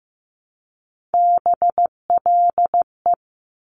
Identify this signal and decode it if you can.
Morse code, heard as a keyed tone, about 15 wpm, 700 Hz. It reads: BLE
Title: Top Three Letter Combinations in Words 15wpm